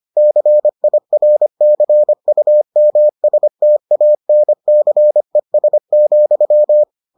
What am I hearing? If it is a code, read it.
Morse code, heard as a keyed tone, about 25 wpm, 600 Hz. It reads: CIRCUMSTANCES,